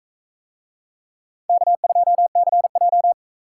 Morse code, heard as a keyed tone, about 39 wpm, 700 Hz. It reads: K2CJ